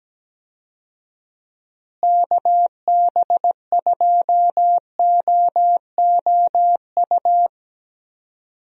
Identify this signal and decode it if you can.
Morse code, heard as a keyed tone, about 17 wpm, 700 Hz. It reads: KB2OOU
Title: USA Call Signs - 6 Characters - Repeated 2x - 17wpm